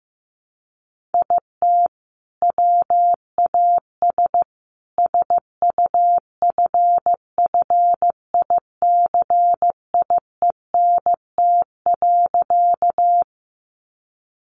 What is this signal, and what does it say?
Morse code, heard as a keyed tone, about 15 wpm, 700 Hz. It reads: IT WAS SUFFICIENT.
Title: Sentences from Top 1400 Words 15wpm